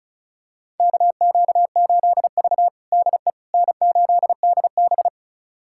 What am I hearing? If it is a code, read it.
Morse code, heard as a keyed tone, about 36 wpm, 700 Hz. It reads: KQ8V DE N8DB
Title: USA Callsign DE Callsign - 4 Characters - 35wpm